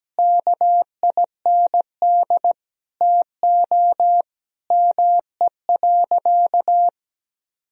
Morse code, heard as a keyed tone, about 17 wpm, 700 Hz. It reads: KIND TO ME.
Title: Sentences from Top 200 Words and Abbreviations 17wpm